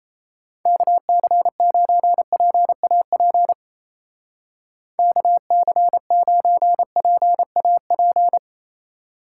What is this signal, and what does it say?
Morse code, heard as a keyed tone, about 34 wpm, 700 Hz. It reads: KC9PAP KC9PAP
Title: Speed Racing - USA 2x3 Call Signs 28wpm